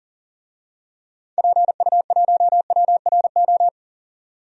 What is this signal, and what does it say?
Morse code, heard as a keyed tone, about 40 wpm, 700 Hz. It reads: PU1WRO
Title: DXCC-100 Call Signs - 6 Characters - 40wpm